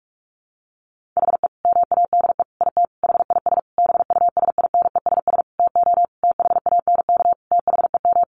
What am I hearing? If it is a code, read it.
Morse code, heard as a keyed tone, about 44 wpm, 700 Hz. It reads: HE MADE IT HIS BUSINESS TO THANK THEM
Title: Sentences from Top 1400 Words 45wpm